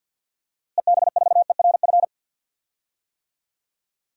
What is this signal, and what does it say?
Morse code, heard as a keyed tone, about 53 wpm, 700 Hz. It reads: EB4ERF
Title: DXCC-100 Call Signs - 6 Characters - Repeated 2x - 50wpm